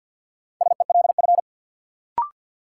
Morse code, heard as a keyed tone, about 48 wpm, 700 Hz. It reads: SELF
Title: Sentences from Top 1400 Words - New Word Drill Alt - 50wpm